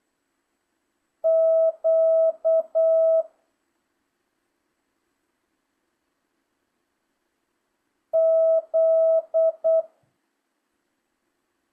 Morse code, heard as a keyed tone, about 8 wpm, 660 Hz. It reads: Q Z